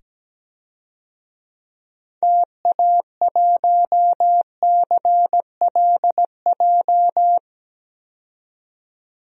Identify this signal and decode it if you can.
Morse code, heard as a keyed tone, about 17 wpm, 700 Hz. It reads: TA1CLJ
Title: DXCC-100 Call Signs - 6 Characters - 17wpm